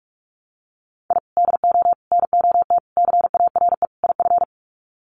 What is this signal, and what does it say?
Morse code, heard as a keyed tone, about 44 wpm, 700 Hz. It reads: I DO NOT CARE IF
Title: Sets of 5 Words 45wpm